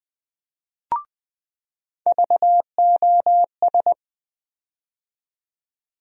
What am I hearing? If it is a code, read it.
Morse code, heard as a keyed tone, about 21 wpm, 700 Hz. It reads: VOS
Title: USA Call Sign Postfixes - 3 Letters 20wpm